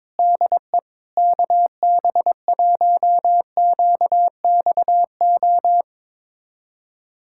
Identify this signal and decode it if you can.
Morse code, heard as a keyed tone, about 22 wpm, 700 Hz. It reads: DE KB1QXO
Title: USA Callsign DE Callsign - 6 Characters - 22wpm